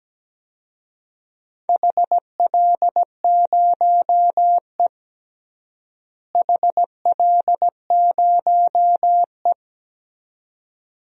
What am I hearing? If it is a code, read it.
Morse code, heard as a keyed tone, about 17 wpm, 700 Hz. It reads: HL0E HL0E
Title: DXCC-100 Call Signs - 4 Characters - Repeated 2x - 17wpm